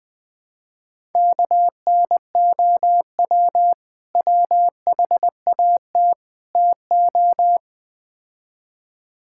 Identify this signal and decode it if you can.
Morse code, heard as a keyed tone, about 20 wpm, 700 Hz. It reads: KNOW WHAT TO
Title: Build A Phrase - 5-Word Phrases - 20wpm